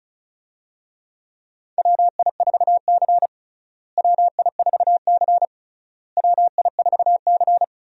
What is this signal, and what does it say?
Morse code, heard as a keyed tone, about 36 wpm, 700 Hz. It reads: WI4C WI4C WI4C
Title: USA Call Signs - 4 Characters - Repeated 3x - 35wpm